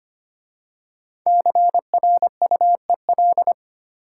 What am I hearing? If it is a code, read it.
Morse code, heard as a keyed tone, about 25 wpm, 700 Hz. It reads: CRUEL
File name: 5 Letter Words - First 3 Letters Given - 25wpm